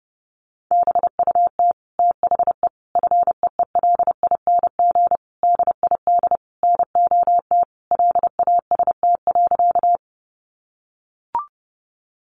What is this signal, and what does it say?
Morse code, heard as a keyed tone, about 30 wpm, 700 Hz. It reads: BUT THE FEELING DID NOT LAST.
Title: Sentences from Top 1200 Words 30wpm